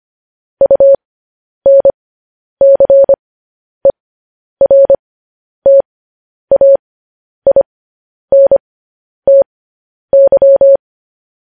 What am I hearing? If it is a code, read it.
Morse code, heard as a keyed tone, about 25 wpm, spaced as for 10 wpm, 550 Hz. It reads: UNCERTAINTY